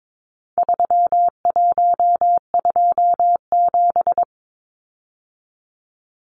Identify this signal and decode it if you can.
Morse code, heard as a keyed tone, about 22 wpm, 700 Hz. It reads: 3127
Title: Numbers - 4-Digit 22wpm